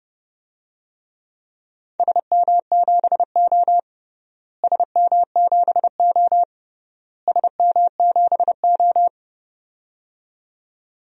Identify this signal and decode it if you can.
Morse code, heard as a keyed tone, about 30 wpm, 700 Hz. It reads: SM7O SM7O SM7O
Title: DXCC-100 Call Signs - 4 Characters - Repeated 3x - 30wpm